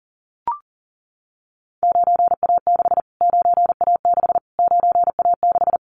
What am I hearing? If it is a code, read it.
Morse code, heard as a keyed tone, about 37 wpm, 700 Hz. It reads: 9A6 9A6 9A6
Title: Callsign Prefixes for Top 25 DXCC - Repeated 3x - 40wpm